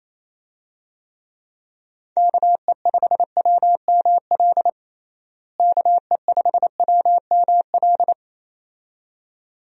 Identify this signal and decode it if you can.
Morse code, heard as a keyed tone, about 28 wpm, 700 Hz. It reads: KE5WML KE5WML
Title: USA Call Signs - 6 Characters - Repeated 2x - 28wpm